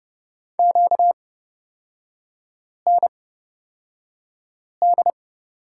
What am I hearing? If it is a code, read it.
Morse code, heard as a keyed tone, about 29 wpm, spaced as for 5 wpm, 690 Hz. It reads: QND
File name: Lesson 38e - Letter q - Three Characters 30wpm